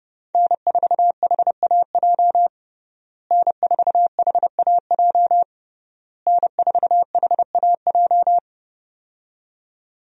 Morse code, heard as a keyed tone, about 30 wpm, 700 Hz. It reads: N4HAJ N4HAJ N4HAJ
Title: USA Call Signs - 5 Characters - Repeated 3x - 30wpm